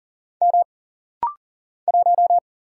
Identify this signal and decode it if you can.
Morse code, heard as a keyed tone, about 40 wpm, 700 Hz. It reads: M 1
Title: Single Letter-Number - Rapid-Fire 40wpm